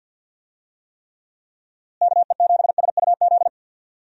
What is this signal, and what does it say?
Morse code, heard as a keyed tone, about 52 wpm, 700 Hz. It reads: KE7SUZ
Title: USA Call Signs - 6 Characters - Repeated 2x - 50wpm